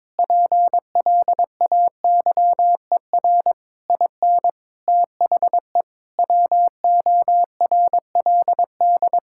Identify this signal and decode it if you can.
Morse code, heard as a keyed tone, about 22 wpm, 700 Hz. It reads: PLAYER IN THE WORLD
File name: practice / 22wpm / Sets of 6 Words - 22wpm